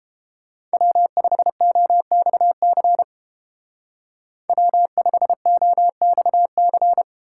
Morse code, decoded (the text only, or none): W5OXC W5OXC